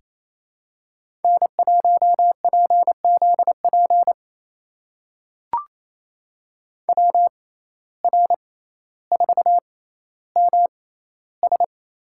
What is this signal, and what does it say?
Morse code, heard as a keyed tone, about 28 wpm, spaced as for 14 wpm, 700 Hz. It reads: N1PZP WR4MS
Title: USA Call Signs - 5 Characters - 6x Character Spacing - 28wpm